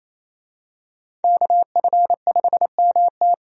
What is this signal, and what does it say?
Morse code, heard as a keyed tone, about 27 wpm, 700 Hz. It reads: KF5MT
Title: USA Call Signs - 5 Characters - 28wpm